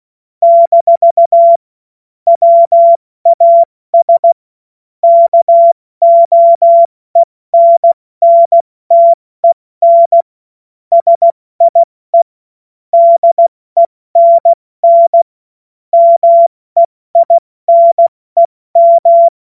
Morse code, heard as a keyed tone, about 16 wpm, 680 Hz. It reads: - WAS KOENNTEN SIE DENN MEINEM
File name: Morsekurs / Witze / Witz 20. Dezember 2022 70 F80